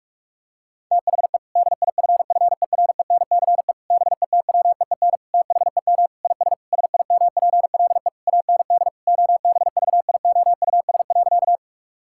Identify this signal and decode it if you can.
Morse code, heard as a keyed tone, about 48 wpm, 700 Hz. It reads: THE DIFFERENCE BETWEEN THEM IS SIMPLE AND OBVIOUS.
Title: Sentences from Top 1600 Words 45wpm